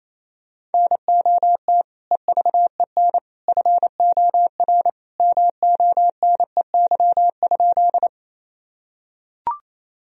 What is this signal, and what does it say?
Morse code, heard as a keyed tone, about 28 wpm, 700 Hz. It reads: NOT EVEN FOR MONEY?
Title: Sentences from Top 1000 Words - Review Part 1 - 28wpm